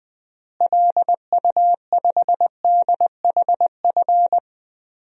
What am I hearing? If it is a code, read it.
Morse code, heard as a keyed tone, about 20 wpm, 700 Hz. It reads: LU5DHF
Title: DXCC-100 Call Signs - 6 Characters - Repeated 3x - 20wpm